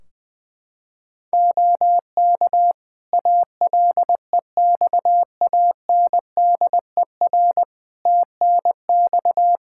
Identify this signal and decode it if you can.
Morse code, heard as a keyed tone, about 21 wpm, 700 Hz. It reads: OK ALEXANDER TNX
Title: K1USN SST QSO Practice 20wpm